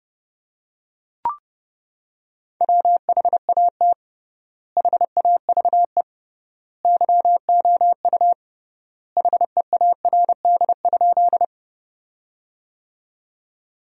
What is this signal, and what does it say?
Morse code, heard as a keyed tone, about 32 wpm, 700 Hz. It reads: WHAT HAVE YOU HEARD?
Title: 3x Word-Spacing - Sentences from Top 400 Words - 30wpm